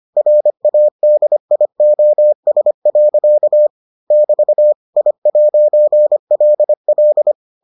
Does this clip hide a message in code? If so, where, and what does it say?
Yes: Morse code from 0.2 to 7.3 s, heard as a keyed tone, about 25 wpm, 600 Hz. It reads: RADIOS. <BT> I'LL